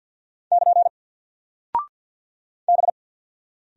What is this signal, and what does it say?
Morse code, heard as a keyed tone, about 48 wpm, 700 Hz. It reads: <KN> B
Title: Single Character (Letter:Number:Punctuation:Prosign) Alt 50wpm